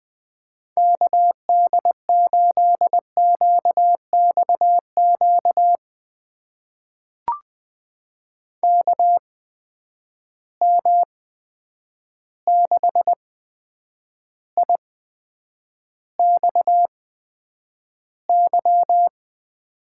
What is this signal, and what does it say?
Morse code, heard as a keyed tone, about 20 wpm, spaced as for 9 wpm, 700 Hz. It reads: KD8QXQ KM6IXY
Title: USA Call Signs - 6 Characters - 8x Character Spacing - 20wpm